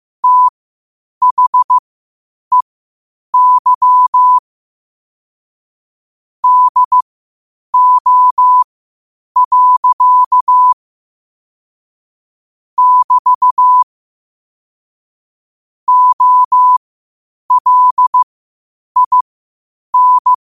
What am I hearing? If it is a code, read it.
Morse code, heard as a keyed tone, about 15 wpm, spaced as for 8 wpm, 1000 Hz. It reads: THEY DO. <BT> OLIN
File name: qotdfiles / 8wpm